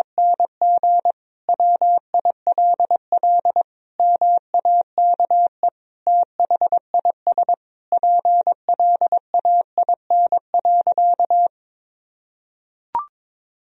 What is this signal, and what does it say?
Morse code, heard as a keyed tone, about 22 wpm, 700 Hz. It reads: NG WILL MAKE THIS PLAIN.
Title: Sentences from Top 1000 Words - Review Part 4 - 22wpm